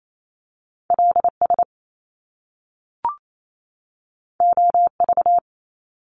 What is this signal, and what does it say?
Morse code, heard as a keyed tone, about 28 wpm, 700 Hz. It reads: L S O V
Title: Top Two Letter Combinations in Words 28wpm